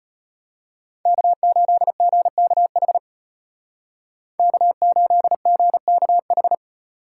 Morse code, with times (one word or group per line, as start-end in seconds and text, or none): 1.1-3.0 K8GKH
4.4-6.5 K8GKH